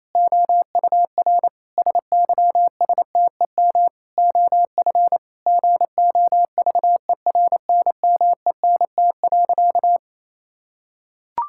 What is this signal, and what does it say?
Morse code, heard as a keyed tone, about 28 wpm, 700 Hz. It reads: OUR SYSTEM OF GOVERNMENT.
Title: Sentences from Top 1000 Words - Review Part 6 - 28wpm